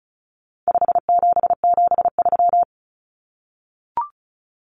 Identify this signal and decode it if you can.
Morse code, heard as a keyed tone, about 34 wpm, 700 Hz. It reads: 5773
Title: Common Cut Numbers - 4-Digit 35wpm